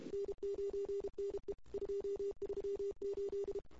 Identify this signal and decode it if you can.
Morse code, heard as a keyed tone, about 32 wpm, 410 Hz. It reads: N9NE 238
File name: CW / 2008-ss-cw-k5tr-op-n5rz -23:00